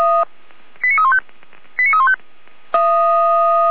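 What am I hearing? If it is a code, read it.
Five-tone selective call (ZVEI), heard as quick runs of five tones, tones of about 70 ms. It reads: 89315 89315
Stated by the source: Five-tone ZVEI selective call received by SDR.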